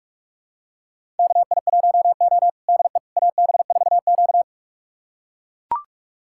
Morse code, heard as a keyed tone, about 42 wpm, 700 Hz. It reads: KI1O DE AB4Q
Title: USA Callsign DE Callsign - 4 Characters - 45wpm